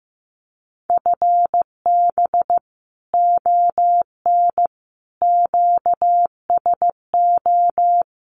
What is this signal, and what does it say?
Morse code, heard as a keyed tone, about 15 wpm, 700 Hz. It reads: FB ON QSO
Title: Common QSO Phrases 15wpm